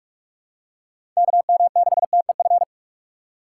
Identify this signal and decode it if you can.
Morse code, heard as a keyed tone, about 42 wpm, 700 Hz. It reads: KM6TEF
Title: USA Call Signs - 6 Characters - 3x Character Spacing - 45wpm